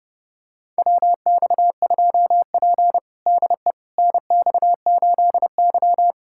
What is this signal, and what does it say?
Morse code, heard as a keyed tone, about 32 wpm, 700 Hz. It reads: WX2P DE NX8Y
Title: USA Callsign DE Callsign - 4 Characters - 30wpm